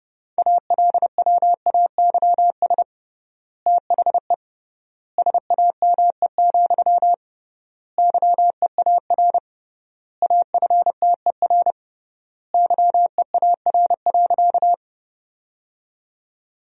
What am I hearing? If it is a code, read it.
Morse code, heard as a keyed tone, about 30 wpm, 700 Hz. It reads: ALWAYS THE SAME, YEAR AFTER YEAR.
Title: 3x Word-Spacing - Sentences from Top 300 Words - 30wpm